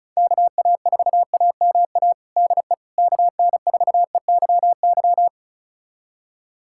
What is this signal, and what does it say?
Morse code, heard as a keyed tone, about 34 wpm, 700 Hz. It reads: KA4AMA DE KN4EYY
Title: USA Callsign DE Callsign 35wpm